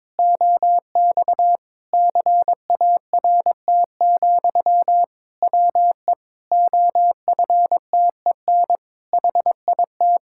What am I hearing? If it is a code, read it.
Morse code, heard as a keyed tone, about 21 wpm, 700 Hz. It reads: OX CART, WE OFTEN HIT
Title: Pangram Challenge v2 - 22wpm